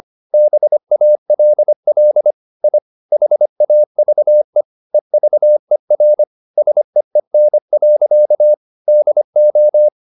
Morse code, heard as a keyed tone, about 25 wpm, 600 Hz. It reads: BALL I HAVE EVER SEEN. DO
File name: the old man and the sea part 1 25wpm